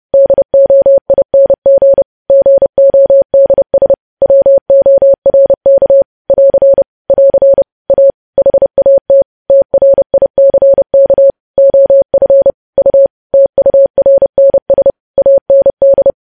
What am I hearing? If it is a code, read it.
Morse code, heard as a keyed tone, about 30 wpm, 550 Hz. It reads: DOING GODS WORK <AR> <AR> A HAT TRICK OF U TURNS AND
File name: podcast / News headlines-1x-30x30